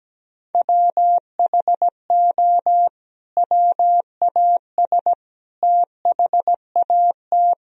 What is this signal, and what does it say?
Morse code, heard as a keyed tone, about 17 wpm, 700 Hz. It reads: WHO WAS THAT
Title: Sentences from Top 1000 Words - Review Part 4 - 17wpm